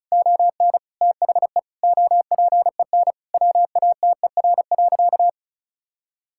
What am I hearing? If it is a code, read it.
Morse code, heard as a keyed tone, about 36 wpm, 700 Hz. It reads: ON THE OPEN WATER.
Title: Sentences from Top 1000 Words - Review Part 6 - 35wpm